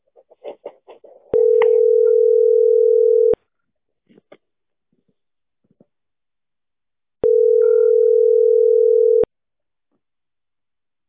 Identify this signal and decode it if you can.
Ringback tone (North America), 440 + 480 Hz.